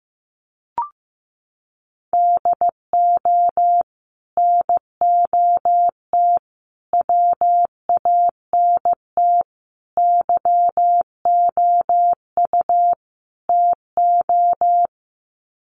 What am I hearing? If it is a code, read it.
Morse code, heard as a keyed tone, about 15 wpm, 700 Hz. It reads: DO NOT WANT YOU TO
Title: Sets of 5 Words 15wpm